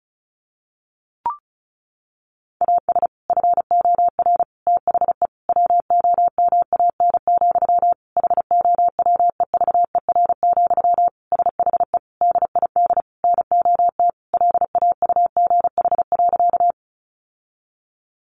Morse code, heard as a keyed tone, about 34 wpm, 700 Hz. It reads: AS FOR THE WOMAN, HOWEVER, SHE DID NOT LAUGH.
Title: Sentences from Top 1100 Words 35wpm